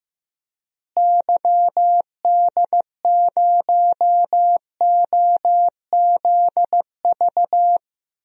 Morse code, heard as a keyed tone, about 15 wpm, 700 Hz. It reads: YD0OZV
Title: DXCC-100 Call Signs - 6 Characters - Repeated 2x - 15wpm